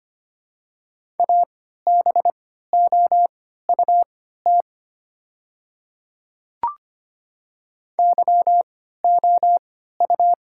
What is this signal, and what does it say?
Morse code, heard as a keyed tone, about 25 wpm, spaced as for 14 wpm, 700 Hz. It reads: ABOUT YOU
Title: 3x Character Spacing - Top 100 Words - 25wpm